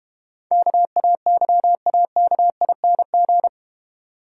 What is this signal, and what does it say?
Morse code, heard as a keyed tone, about 31 wpm, 700 Hz. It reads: KAYAKING